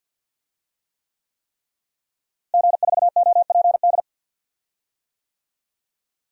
Morse code, heard as a keyed tone, about 46 wpm, 700 Hz. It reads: G4OPD
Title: DXCC-100 Call Signs - 5 Characters - Repeated 3x - 50wpm